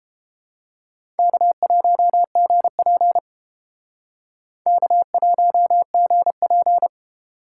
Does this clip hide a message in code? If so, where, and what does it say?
Yes: Morse code from 1.2 to 6.9 s, heard as a keyed tone, about 31 wpm, 700 Hz. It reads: K1GP K1GP